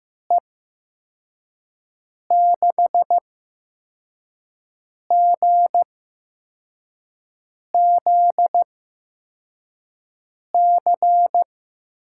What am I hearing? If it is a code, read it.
Morse code, heard as a keyed tone, about 15 wpm, spaced as for 4 wpm, 700 Hz. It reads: E6GZC
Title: USA Call Signs - 8x Character Spacing - 15wpm